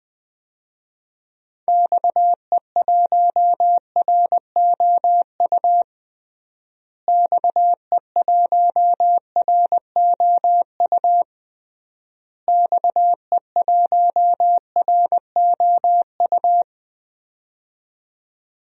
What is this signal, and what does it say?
Morse code, heard as a keyed tone, about 20 wpm, 700 Hz. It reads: XE1ROU XE1ROU XE1ROU
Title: DXCC-100 Call Signs - 6 Characters - Repeated 3x - 20wpm